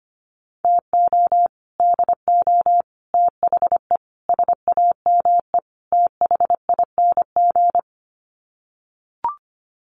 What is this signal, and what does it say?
Morse code, heard as a keyed tone, about 25 wpm, 690 Hz. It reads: TO DO THE SAME THING
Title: Sets of 5 Words Encore 25wpm